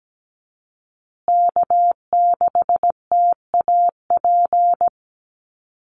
Morse code, heard as a keyed tone, about 17 wpm, 700 Hz. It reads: K6TAP
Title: USA Call Signs Repeated 2x 17wpm